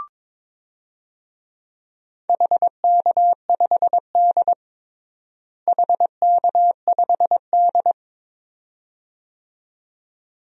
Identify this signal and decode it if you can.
Morse code, heard as a keyed tone, about 22 wpm, 700 Hz. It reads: HK5D HK5D
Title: DXCC-100 Call Signs - 4 Characters - Repeated 2x - 22wpm